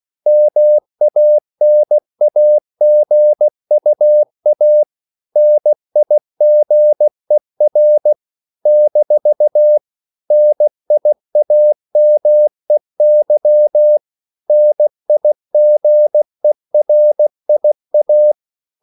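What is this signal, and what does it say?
Morse code, heard as a keyed tone, about 16 wpm, 600 Hz. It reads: MANAGUA NIGER - NIAMEY NIGERIA
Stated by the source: CW practice of national capitals at 16 words per minute, from ON6ZQ.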